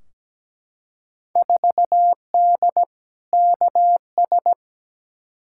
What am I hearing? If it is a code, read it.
Morse code, heard as a keyed tone, about 17 wpm, 700 Hz. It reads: 4D KS
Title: Field Day Exchanges Encore - 17wpm